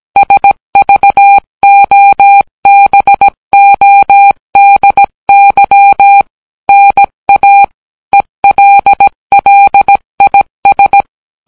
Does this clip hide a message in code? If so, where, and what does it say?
Yes: Morse code from 0.2 to 11.0 s, heard as a keyed tone, about 17 wpm, 800 Hz. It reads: SVOBODY NA ELLIS